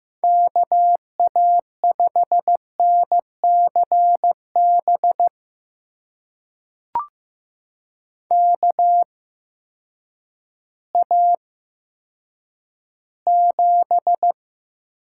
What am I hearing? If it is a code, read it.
Morse code, heard as a keyed tone, about 15 wpm, spaced as for 9 wpm, 700 Hz. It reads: KA5NCB KA7
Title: USA Call Signs - 6 Characters - 8x Character Spacing - 15wpm